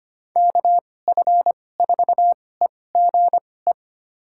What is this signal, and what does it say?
Morse code, heard as a keyed tone, about 25 wpm, spaced as for 18 wpm, 700 Hz. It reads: KF4EGE
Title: USA Call Signs - 6 Characters - 2x Character Spacing - 25wpm